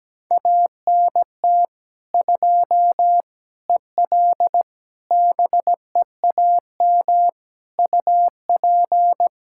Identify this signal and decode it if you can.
Morse code, heard as a keyed tone, about 17 wpm, 700 Hz. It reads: ANT 2 EL BEAM UP
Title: Practice QSO - Exchange 2 - Info - 17wpm